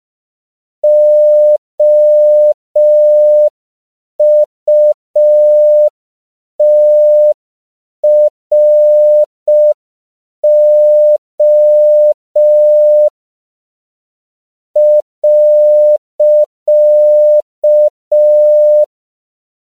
Morse code, heard as a keyed tone, about 5 wpm, 600 Hz. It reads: OUTRO .
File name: 5PPM